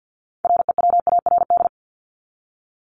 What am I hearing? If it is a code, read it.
Morse code, heard as a keyed tone, about 48 wpm, 700 Hz. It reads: REWARD
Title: Top 1751-2000 Words 50wpm